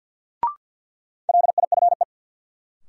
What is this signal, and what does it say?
Morse code, heard as a keyed tone, about 53 wpm, 700 Hz. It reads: LIFE
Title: Sentences from Top 300 Words - New Word Drill - 50wpm